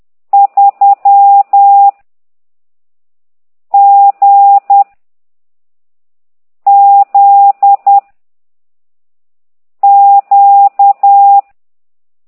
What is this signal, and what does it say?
Morse code, heard as a keyed tone, about 10 wpm, 800 Hz. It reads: 3 G Z Q